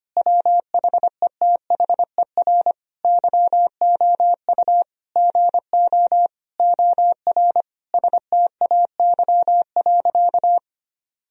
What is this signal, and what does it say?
Morse code, heard as a keyed tone, about 25 wpm, 700 Hz. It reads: WHETHER YOU GO OR STAY.
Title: Sentences from Top 1000 Words - Review Part 3 - 25wpm